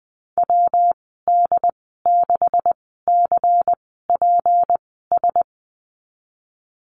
Morse code, heard as a keyed tone, about 20 wpm, spaced as for 14 wpm, 700 Hz. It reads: WD6CPS
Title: 193 Lesson 31g - Letter v - Call Signs 20wpm